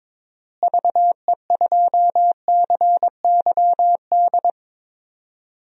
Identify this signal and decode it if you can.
Morse code, heard as a keyed tone, about 22 wpm, 700 Hz. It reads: VE2CYD
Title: DXCC-100 Call Signs - 6 Characters - Repeated 2x - 22wpm